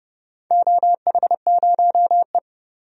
Morse code, heard as a keyed tone, about 30 wpm, 700 Hz. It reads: OH0E
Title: DXCC-100 Call Signs - 4 Characters - 30wpm